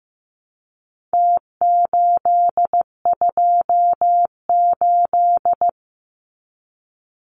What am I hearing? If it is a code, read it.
Morse code, heard as a keyed tone, about 15 wpm, 700 Hz. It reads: T828
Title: Numbers - Contest Style - 4-Digit 15wpm